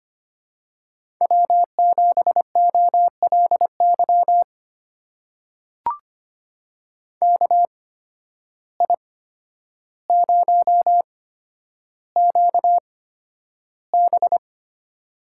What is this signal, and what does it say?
Morse code, heard as a keyed tone, about 25 wpm, spaced as for 11 wpm, 700 Hz. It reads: W7OLY KI0QB
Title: USA Call Signs - 8x Character Spacing - 25wpm